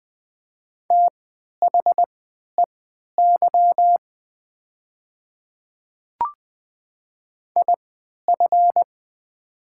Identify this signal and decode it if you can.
Morse code, heard as a keyed tone, about 20 wpm, spaced as for 11 wpm, 700 Hz. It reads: THEY IF
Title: 3x Character Spacing - Top 100 Words - 20wpm